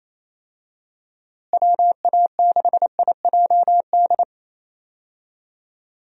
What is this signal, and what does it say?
Morse code, heard as a keyed tone, about 28 wpm, 700 Hz. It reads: WA6IJD
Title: USA Call Signs - 6 Characters - Repeated 3x - 28wpm